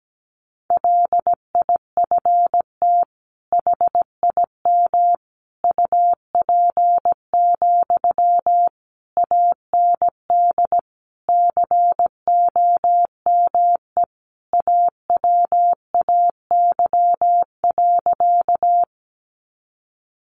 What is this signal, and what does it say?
Morse code, heard as a keyed tone, about 17 wpm, 700 Hz. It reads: LIFT HIM UP, AND COME AWAY.
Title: Sentences from Top 1000 Words 17wpm